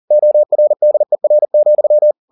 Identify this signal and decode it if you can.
Morse code, heard as a keyed tone, about 40 wpm, 600 Hz. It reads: ORDER,
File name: the old man and the sea 2 40wpm